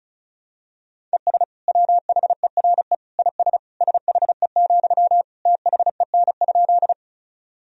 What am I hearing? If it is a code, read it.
Morse code, heard as a keyed tone, about 36 wpm, 690 Hz. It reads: ES WHERE IS SHE, THEN?
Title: Sentences from Top 200 Words and Abbreviations 35wpm